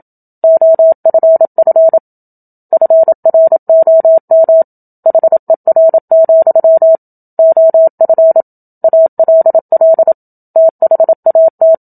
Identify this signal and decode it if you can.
Morse code, heard as a keyed tone, about 27 wpm, 650 Hz. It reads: OFF FROM HER, OF ALL THAT
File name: Book Two / 01-Chapter 1 - 27 WPM